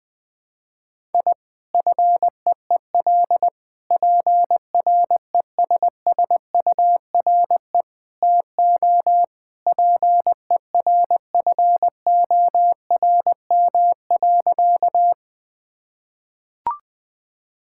Morse code, heard as a keyed tone, about 21 wpm, 700 Hz. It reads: I FEEL PRESSURE TO PERFORM.
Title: Sentences from Top 1000 Words - Review Part 6 - 20wpm